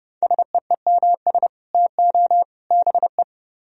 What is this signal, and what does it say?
Morse code, heard as a keyed tone, about 28 wpm, 700 Hz. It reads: SEEMS TO BE